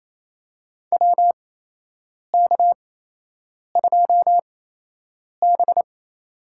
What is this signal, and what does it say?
Morse code, heard as a keyed tone, about 28 wpm, spaced as for 8 wpm, 700 Hz. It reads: WK2B
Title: USA Call Signs - 8x Character Spacing - 28wpm